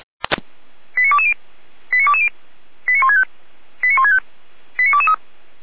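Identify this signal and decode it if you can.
Five-tone selective call (ZVEI), heard as quick runs of five tones, tones of about 70 ms. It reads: 89220 89220 89165 89165 89223
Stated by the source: ZVEI five-tone selective call 89119 received on an SDR.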